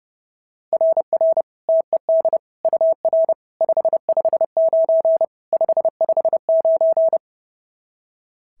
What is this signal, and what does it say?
Morse code, heard as a keyed tone, about 32 wpm, 690 Hz. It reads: RR TED UR 559 559
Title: SOTA QSO Practice 30wpm